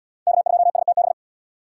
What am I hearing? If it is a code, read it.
Morse code, heard as a keyed tone, about 68 wpm, 700 Hz. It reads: K2STB